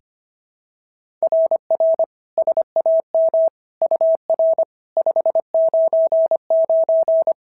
Morse code, heard as a keyed tone, about 24 wpm, 650 Hz. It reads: RR SAM UR 599
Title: SOTA QSO Practice 25wpm